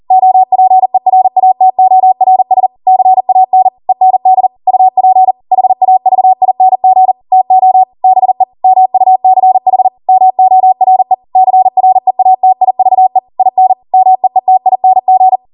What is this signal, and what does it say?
Morse code, heard as a keyed tone, about 40 wpm, 750 Hz. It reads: OPERATORS CAN END UP HAVING TO BE MUCH MORE CREATIVE IN MEETING